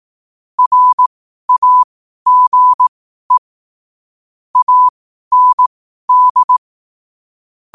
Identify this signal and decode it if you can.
Morse code, heard as a keyed tone, about 18 wpm, spaced as for 12 wpm, 1000 Hz. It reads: RAGE AND
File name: 12wpm